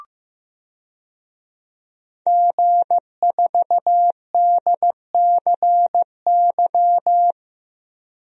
Morse code, heard as a keyed tone, about 15 wpm, 700 Hz. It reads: G4DCY
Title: DXCC-100 Call Signs - 5 Characters - 15wpm